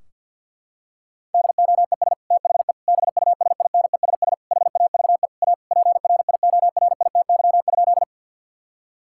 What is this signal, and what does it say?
Morse code, heard as a keyed tone, about 52 wpm, 700 Hz. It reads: DOES THE BUSINESS HAVE A PRIORITY?
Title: Sentences from Top 1600 Words 50wpm